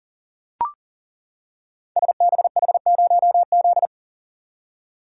Morse code, heard as a keyed tone, about 40 wpm, 690 Hz. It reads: SBH0Z
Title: Code Groups - 5 Letter:Numbers - 40wpm